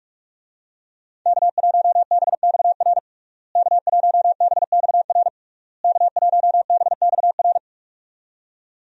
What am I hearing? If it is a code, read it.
Morse code, heard as a keyed tone, about 46 wpm, 700 Hz. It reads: K1BXR K1BXR K1BXR
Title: USA Call Signs - 5 Characters - Repeated 3x - 45wpm